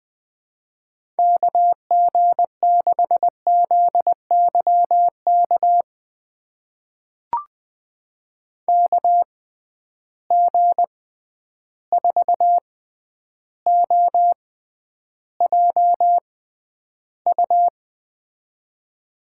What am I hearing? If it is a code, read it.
Morse code, heard as a keyed tone, about 21 wpm, spaced as for 10 wpm, 700 Hz. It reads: KG6ZYK KG4OJU
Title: USA Call Signs - 6 Characters - 6x Character Spacing - 20wpm